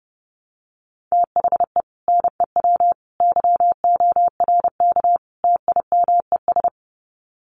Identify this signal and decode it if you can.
Morse code, heard as a keyed tone, about 30 wpm, 700 Hz. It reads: THE NEW YORK TIMES
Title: Sets of 4 Words 30wpm